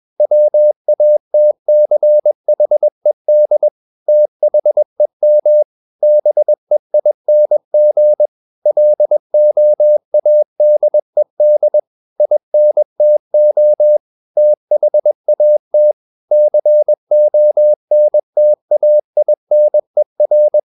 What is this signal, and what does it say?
Morse code, heard as a keyed tone, about 21 wpm, 600 Hz. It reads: WATCHED THEM BEING LOADED INTO THAT CONTAINER
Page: CW med Gnister